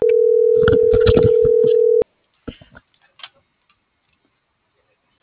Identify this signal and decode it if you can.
Ringback tone (North America), 440 + 480 Hz.